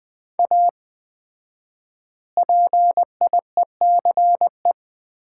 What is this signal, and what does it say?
Morse code, heard as a keyed tone, about 20 wpm, 700 Hz. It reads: A PIECE
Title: 4x Word-Spacing - Sets of 3 Words - 20wpm